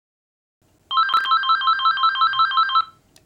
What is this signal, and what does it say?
There is no code.